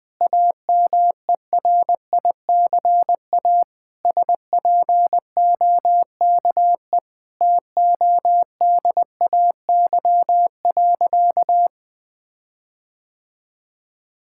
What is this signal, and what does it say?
Morse code, heard as a keyed tone, about 20 wpm, 700 Hz. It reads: AMERICA SPOKE TODAY.
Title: Sentences with 8-Word Phrases - 20wpm